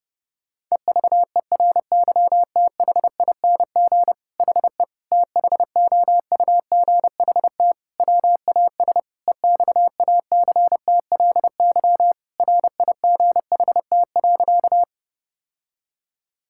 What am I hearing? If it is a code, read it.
Morse code, heard as a keyed tone, about 30 wpm, 700 Hz. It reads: EVERYTHING HE THOUGHT WAS EXACTLY RIGHT.
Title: Sentences from Top 1000 Words - Review Part 6 - 30wpm